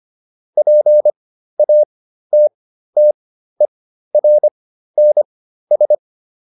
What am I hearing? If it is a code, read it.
Morse code, heard as a keyed tone, about 25 wpm, spaced as for 13 wpm, 600 Hz. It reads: PATTERNS